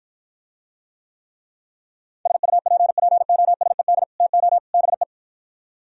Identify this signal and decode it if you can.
Morse code, heard as a keyed tone, about 52 wpm, 700 Hz. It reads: SUPPOSED TO BE